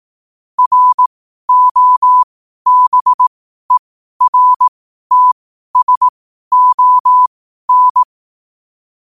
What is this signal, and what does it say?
Morse code, heard as a keyed tone, about 18 wpm, spaced as for 12 wpm, 1000 Hz. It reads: ROBERTSON
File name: qotdfiles / 12wpm